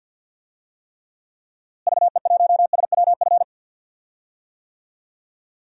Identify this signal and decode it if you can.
Morse code, heard as a keyed tone, about 52 wpm, 700 Hz. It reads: VE1SWF
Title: DXCC-100 Call Signs - 6 Characters - Repeated 2x - 50wpm